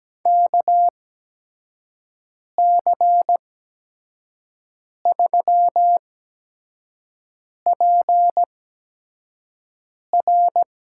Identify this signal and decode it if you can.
Morse code, heard as a keyed tone, about 17 wpm, spaced as for 5 wpm, 700 Hz. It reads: KC3PR
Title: USA Call Signs - 5 Characters - 8x Character Spacing - 17wpm